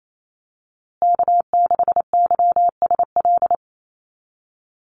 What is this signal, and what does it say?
Morse code, heard as a keyed tone, about 28 wpm, 700 Hz. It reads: K6YSL
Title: USA Call Signs Repeated 2x 28wpm